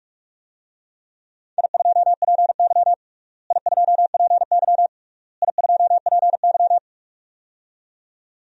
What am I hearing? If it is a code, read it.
Morse code, heard as a keyed tone, about 46 wpm, 700 Hz. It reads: I2PY I2PY I2PY
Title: DXCC-100 Call Signs - 4 Characters - Repeated 3x - 45wpm